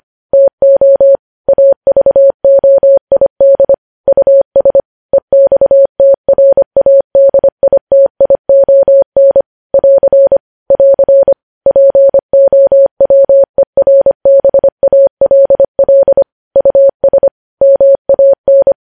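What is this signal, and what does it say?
Morse code, heard as a keyed tone, about 25 wpm, 550 Hz. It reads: TO AVOID US EXTRADITION <AR> <AR> POWERBALL US MAN
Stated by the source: Morse code news headlines sent at 25 wpm.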